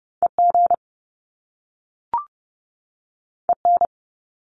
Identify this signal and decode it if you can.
Morse code, heard as a keyed tone, about 30 wpm, 700 Hz. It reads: EG EN
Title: Top Two Letter Combinations in Words 30wpm